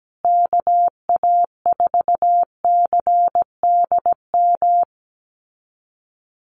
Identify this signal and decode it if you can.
Morse code, heard as a keyed tone, about 17 wpm, 700 Hz. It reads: KA4CDM
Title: USA Call Signs Repeated 2x 17wpm